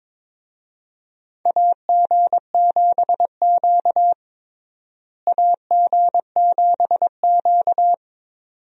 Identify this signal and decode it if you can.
Morse code, heard as a keyed tone, about 22 wpm, 700 Hz. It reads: AG7Q AG7Q
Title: USA Call Signs - 4 Characters - Repeated 2x - 22wpm